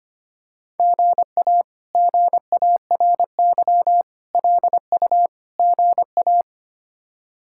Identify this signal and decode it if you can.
Morse code, heard as a keyed tone, about 25 wpm, 700 Hz. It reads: GA GARY LU GA
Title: K1USN SST QSO Practice - 25wpm